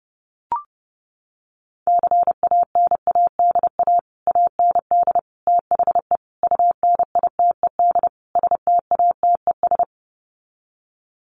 Morse code, heard as a keyed tone, about 30 wpm, 700 Hz. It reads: CANADA AND THE UNITED STATES
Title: Sets of 5 Words Encore 30wpm